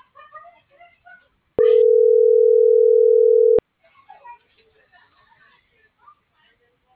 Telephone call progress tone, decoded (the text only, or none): ringback tone